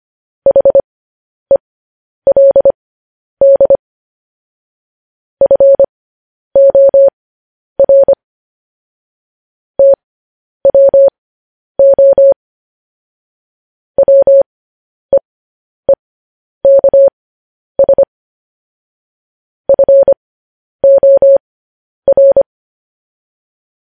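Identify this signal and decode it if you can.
Morse code, heard as a keyed tone, about 25 wpm, spaced as for 10 wpm, 550 Hz. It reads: HELD FOR TWO WEEKS FOR
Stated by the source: Morse code news headlines sent at 10 wpm for practice.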